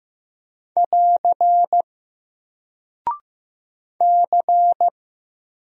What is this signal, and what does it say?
Morse code, heard as a keyed tone, about 15 wpm, 700 Hz. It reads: <AR> C
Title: Single Character (Letter:Number:Punctuation:Prosign) 15wpm